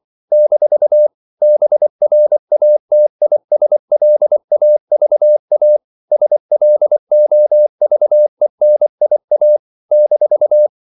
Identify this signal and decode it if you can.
Morse code, heard as a keyed tone, about 24 wpm, 600 Hz. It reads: - BRATISLAVA SLOVENIA -
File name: CW / Practice / NationalCapitals24